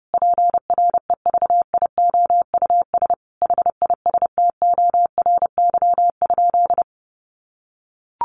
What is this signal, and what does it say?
Morse code, heard as a keyed tone, about 28 wpm, 700 Hz. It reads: PREVIOUS HISTORY?
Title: Sentences from Top 1500 Words 30wpm